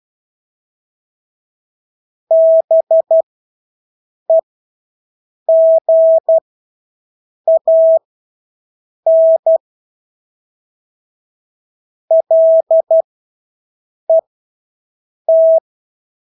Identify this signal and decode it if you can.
Morse code, heard as a keyed tone, about 12 wpm, spaced as for 6 wpm, 650 Hz. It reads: BEGAN LET
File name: Engelske ord 12-6 wpm